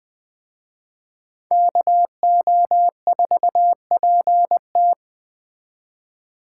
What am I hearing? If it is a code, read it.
Morse code, heard as a keyed tone, about 21 wpm, 700 Hz. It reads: KO4PT
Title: Speed Racing - USA Call Signs 17wpm